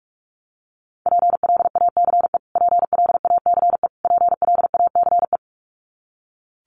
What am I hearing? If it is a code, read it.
Morse code, heard as a keyed tone, about 44 wpm, 700 Hz. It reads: PLACE PLACE PLACE
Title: Top 500 Words Repeated 45wpm